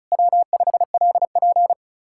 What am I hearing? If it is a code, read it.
Morse code, heard as a keyed tone, about 46 wpm, 700 Hz. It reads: W5LP